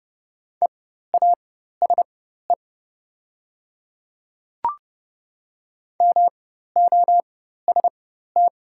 Morse code, heard as a keyed tone, about 30 wpm, spaced as for 14 wpm, 700 Hz. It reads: EASE MOST
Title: 4x Character Spacing - Top 300 Words - 30wpm